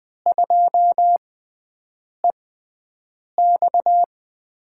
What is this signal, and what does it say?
Morse code, heard as a keyed tone, about 20 wpm, spaced as for 7 wpm, 700 Hz. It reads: 2EX
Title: USA Call Signs - 6x Character Spacing - 20wpm